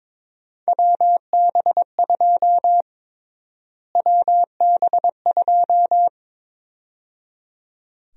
Morse code, heard as a keyed tone, about 21 wpm, 700 Hz. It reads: WB2 WB2
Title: USA Call Sign Prefixes - Repeated 2x - 22wpm